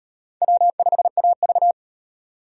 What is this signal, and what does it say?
Morse code, heard as a keyed tone, about 34 wpm, 700 Hz. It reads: W5AV